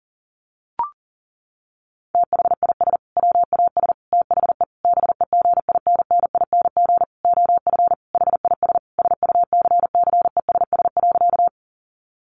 Morse code, heard as a keyed tone, about 40 wpm, 700 Hz. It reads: THIS WAS THE BEGINNING OF HIS SUCCESS.
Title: Sentences from Top 900 Words 40wpm